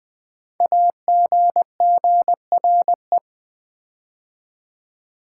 Morse code, heard as a keyed tone, about 20 wpm, 700 Hz. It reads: AGGRE
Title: Build A Word - 5 Letters At A Time - 10-Letter Words - 20wpm